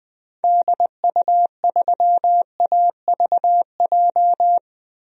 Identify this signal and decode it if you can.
Morse code, heard as a keyed tone, about 21 wpm, 700 Hz. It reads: DU3AVJ
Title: DXCC-100 Call Signs - 6 Characters - Repeated 2x - 20wpm